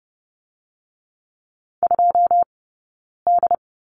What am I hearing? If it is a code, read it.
Morse code, heard as a keyed tone, about 30 wpm, 700 Hz. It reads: 2 D
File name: Lesson 17d - Letter u - Two Characters 30wpm